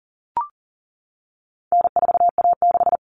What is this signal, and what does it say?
Morse code, heard as a keyed tone, about 40 wpm, 700 Hz. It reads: N4A6
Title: All Cut Numbers - 4-Digit 40wpm